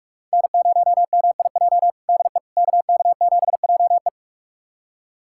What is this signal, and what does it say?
Morse code, heard as a keyed tone, about 41 wpm, 700 Hz. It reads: N0MIJ DE KK7JE
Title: USA Callsign DE Callsign - 5 Characters - 45wpm